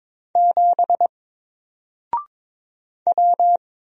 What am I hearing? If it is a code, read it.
Morse code, heard as a keyed tone, about 22 wpm, 700 Hz. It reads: 7 W
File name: Single-Letter-Number-22wpm